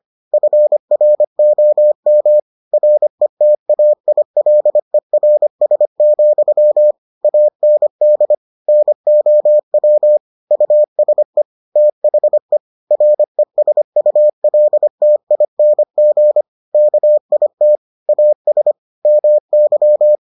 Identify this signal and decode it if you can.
Morse code, heard as a keyed tone, about 25 wpm, 600 Hz. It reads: FROM RETAILERS, AND NOW USE THE RESULTING KIT AS MY